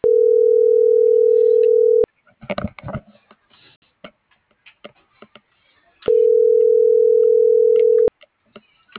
Ringback tone (North America), 440 + 480 Hz.
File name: CENTRAL MAERSK / out kevin.riosmsk